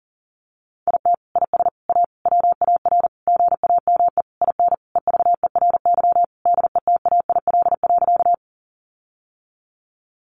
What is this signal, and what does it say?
Morse code, heard as a keyed tone, about 37 wpm, 700 Hz. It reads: IT IS A WAR GAME IN EVERY DETAIL.
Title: Sentences from Top 1500 Words 40wpm